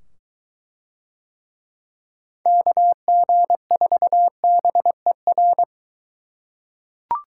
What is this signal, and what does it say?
Morse code, heard as a keyed tone, about 23 wpm, 700 Hz. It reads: KG4BER